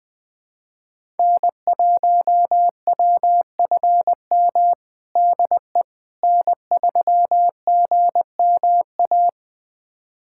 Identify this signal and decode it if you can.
Morse code, heard as a keyed tone, about 20 wpm, 700 Hz. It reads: N1WFM DE N3GMA
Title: USA Callsign DE Callsign - 5 Characters - 20wpm